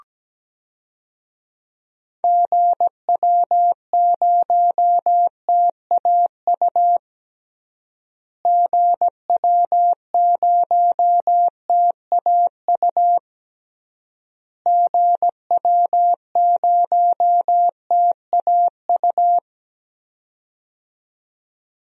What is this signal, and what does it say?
Morse code, heard as a keyed tone, about 17 wpm, 700 Hz. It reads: GW0TAU GW0TAU GW0TAU
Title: DXCC-100 Call Signs - 6 Characters - Repeated 3x - 17wpm